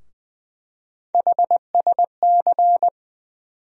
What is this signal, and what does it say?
Morse code, heard as a keyed tone, about 20 wpm, 700 Hz. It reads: HSC
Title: USA Call Sign Postfixes - 3 Letters 20wpm